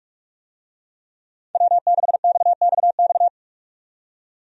Morse code, heard as a keyed tone, about 46 wpm, 700 Hz. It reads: W6XXX
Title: USA Call Signs - 5 Characters - 45wpm